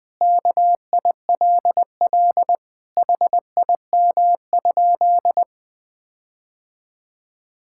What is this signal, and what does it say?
Morse code, heard as a keyed tone, about 20 wpm, 700 Hz. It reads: KILL HIM?
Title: Sentences from Top 1600 Words 20wpm